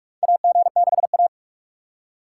Morse code, heard as a keyed tone, about 46 wpm, 700 Hz. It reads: AG6A
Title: USA Call Signs - 4 Characters - 6x Character Spacing - 45wpm